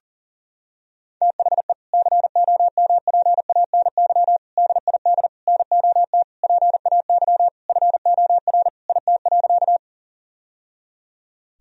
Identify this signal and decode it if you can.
Morse code, heard as a keyed tone, about 40 wpm, 700 Hz. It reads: THE COMPANY DID NOT PAY FOR IT.
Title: Sentences from Top 1000 Words - Review Part 3 - 40wpm